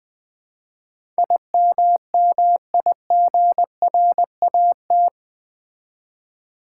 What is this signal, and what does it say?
Morse code, heard as a keyed tone, about 20 wpm, 700 Hz. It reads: IMMIGRAT
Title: Build A Word - 4 Letters At A Time - 11-Letter Words - 20wpm